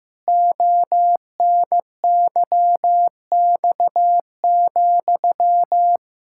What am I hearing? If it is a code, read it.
Morse code, heard as a keyed tone, about 15 wpm, 700 Hz. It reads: ONYX,
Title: Pangram Challenge v2 - 15wpm